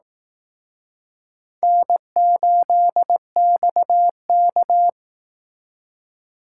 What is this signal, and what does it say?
Morse code, heard as a keyed tone, about 18 wpm, 700 Hz. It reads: N8XK
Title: Speed Racing - USA 1x2 Call Signs 15wpm